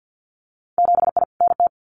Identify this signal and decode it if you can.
Morse code, heard as a keyed tone, about 50 wpm, 700 Hz. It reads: 7I NT
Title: Winter Field Day Exchanges Encore 50wpm